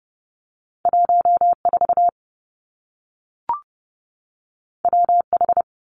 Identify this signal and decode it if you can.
Morse code, heard as a keyed tone, about 30 wpm, 700 Hz. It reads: 1 4 W H
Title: Two Challenging Letter-Number Combinations 30wpm